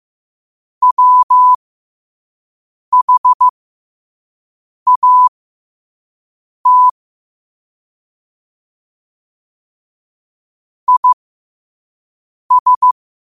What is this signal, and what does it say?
Morse code, heard as a keyed tone, about 15 wpm, spaced as for 5 wpm, 1000 Hz. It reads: WHAT IS
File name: qotdfiles / 5wpm